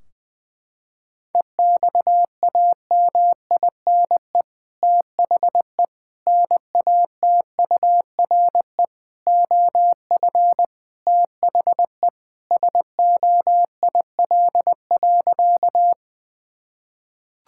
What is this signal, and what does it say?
Morse code, heard as a keyed tone, about 20 wpm, 700 Hz. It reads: EXAMINE THE NATURE OF THE SOIL.
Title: Sentences from Top 1600 Words 20wpm